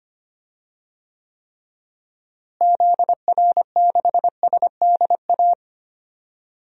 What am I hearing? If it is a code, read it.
Morse code, heard as a keyed tone, about 25 wpm, 700 Hz. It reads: ZR6SDA
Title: DXCC-100 Call Signs - 6 Characters - Repeated 3x - 25wpm